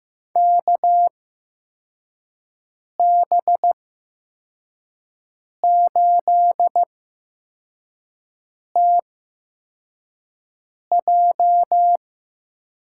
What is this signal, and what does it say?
Morse code, heard as a keyed tone, about 15 wpm, spaced as for 4 wpm, 700 Hz. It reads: KB8TJ
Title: USA Call Signs - 6 Characters - 8x Character Spacing - 15wpm